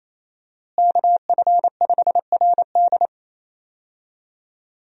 Morse code, heard as a keyed tone, about 28 wpm, 700 Hz. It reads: KF5RD
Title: USA Call Signs - 5 Characters - Repeated 2x - 28wpm